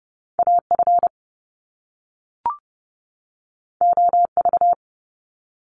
Morse code, heard as a keyed tone, about 30 wpm, 700 Hz. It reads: A F O V
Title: Top Two Letter Combinations in Words 30wpm